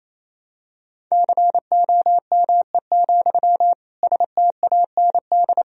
Morse code, heard as a keyed tone, about 28 wpm, 700 Hz. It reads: COME, STAND
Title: Sentences from Top 1000 Words - Review Part 1 - 28wpm